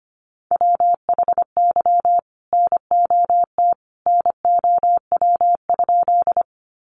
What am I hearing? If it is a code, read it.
Morse code, heard as a keyed tone, about 25 wpm, 700 Hz. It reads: WHY NOT NOW?
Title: Sentences from Top 600 Words 25wpm